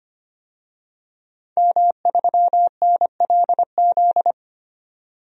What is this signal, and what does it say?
Morse code, heard as a keyed tone, about 25 wpm, 690 Hz. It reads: M3NLZ
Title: DXCC-100 Call Signs - 5 Characters - 25wpm